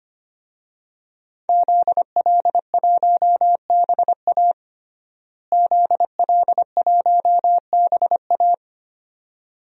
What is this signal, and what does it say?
Morse code, heard as a keyed tone, about 25 wpm, 700 Hz. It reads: ZL1BA ZL1BA
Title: DXCC-100 Call Signs - 5 Characters - Repeated 2x - 25wpm